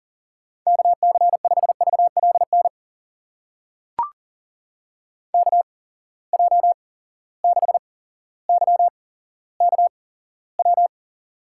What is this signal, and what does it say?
Morse code, heard as a keyed tone, about 39 wpm, spaced as for 17 wpm, 700 Hz. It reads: KC5VLN KJ6YKW
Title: USA Call Signs - 6 Characters - 8x Character Spacing - 40wpm